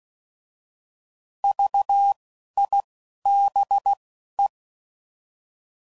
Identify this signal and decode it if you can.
Morse code, heard as a keyed tone, about 16 wpm, 800 Hz. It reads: V I B E
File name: Prefixes / By First Letter / Vs